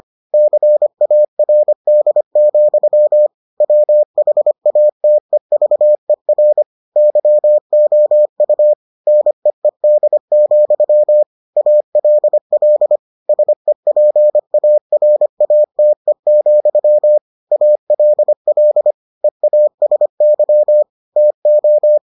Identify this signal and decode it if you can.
Morse code, heard as a keyed tone, about 25 wpm, 600 Hz. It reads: CARD, WHATEVER YOU NEED, ALL SEPARATE, ALL EASY TO